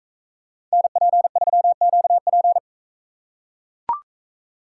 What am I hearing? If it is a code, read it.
Morse code, heard as a keyed tone, about 43 wpm, 700 Hz. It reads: NP3QP